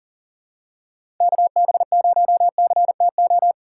Morse code, heard as a keyed tone, about 39 wpm, 700 Hz. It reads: KB0CTO
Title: USA Call Signs - 6 Characters - 2x Character Spacing - 40wpm